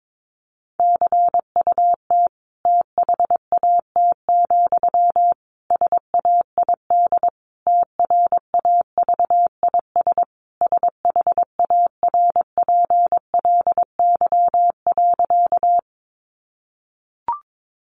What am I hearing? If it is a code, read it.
Morse code, heard as a keyed tone, about 22 wpm, 700 Hz. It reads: CUT THAT, SAID TRAVIS SHARPLY.
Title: A Sound Of Thunder 22wpm